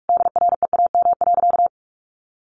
Morse code, heard as a keyed tone, about 45 wpm, 700 Hz. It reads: DREAM.
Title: Sentences from Top 1500 Words 45wpm